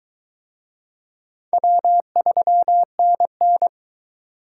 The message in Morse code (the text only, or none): W3NN